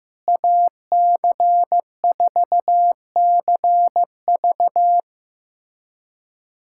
Morse code, heard as a keyed tone, about 15 wpm, 700 Hz. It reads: AC4CV
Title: USA Callsign DE Callsign 15wpm